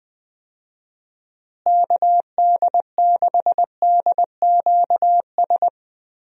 Morse code, heard as a keyed tone, about 20 wpm, 700 Hz. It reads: KD6DQS
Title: USA Call Signs - 6 Characters - Repeated 2x - 20wpm